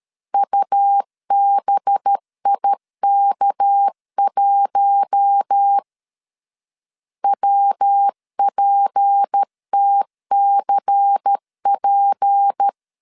Morse code, heard as a keyed tone, about 13 wpm, 800 Hz. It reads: UBIK1 WPTCP
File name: LEZIONI / 6 PKX0 / 6 PK 12wpm 2006